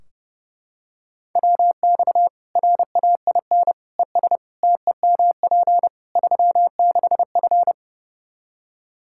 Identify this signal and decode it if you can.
Morse code, heard as a keyed tone, about 32 wpm, 700 Hz. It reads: WX RAIN ES TEMP 36F
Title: Practice QSO - Exchange 2 - Info - 30wpm